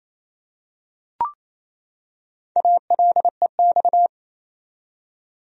Morse code, heard as a keyed tone, about 27 wpm, 700 Hz. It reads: ALEX
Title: USA Common Names - 28wpm